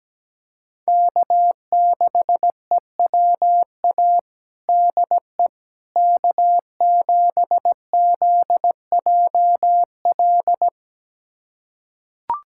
Morse code, heard as a keyed tone, about 17 wpm, 700 Hz. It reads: K6EWA DE K7ZJL
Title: USA Callsign DE Callsign - 5 Characters - 17wpm